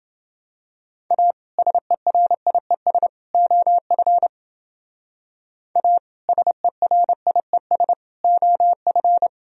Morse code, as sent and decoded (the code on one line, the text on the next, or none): .- / ... .-. .. ... / --- ..-. / .- / ... .-. .. ... / --- ..-.
A SERIES OF A SERIES OF